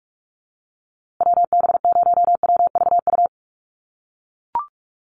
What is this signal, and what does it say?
Morse code, heard as a keyed tone, about 44 wpm, 700 Hz. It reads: WB0WVU
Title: USA Call Signs Repeated 2x 45wpm